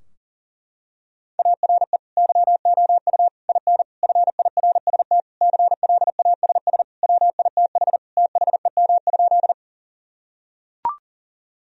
Morse code, heard as a keyed tone, about 40 wpm, 700 Hz. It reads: ARE YOU IN FIRST CLASS WITH THEM?
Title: Sentences from Top 1000 Words - Review Part 2 - 40wpm